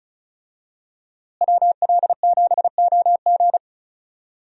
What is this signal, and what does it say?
Morse code, heard as a keyed tone, about 34 wpm, 700 Hz. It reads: WL7OG
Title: USA Call Signs - 5 Characters - Repeated 2x - 35wpm